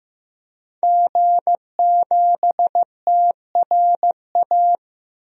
Morse code, heard as a keyed tone, about 15 wpm, 700 Hz. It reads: G7TRA
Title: DXCC-100 Call Signs - 5 Characters - 15wpm